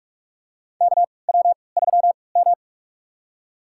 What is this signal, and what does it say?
Morse code, heard as a keyed tone, about 48 wpm, spaced as for 26 wpm, 700 Hz. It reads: KW3M
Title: USA Call Signs - 3x Character Spacing - 45wpm